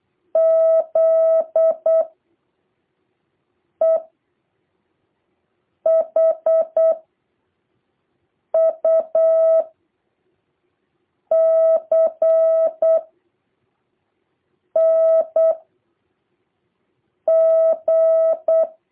Morse code, heard as a keyed tone, about 8 wpm, 650 Hz. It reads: Z E H U C N G
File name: jh8cbh / 2023cw